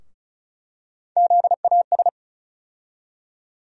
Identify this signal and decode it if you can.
Morse code, heard as a keyed tone, about 36 wpm, 700 Hz. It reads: ZAS
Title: USA Call Sign Postfixes - 3 Letters 35wpm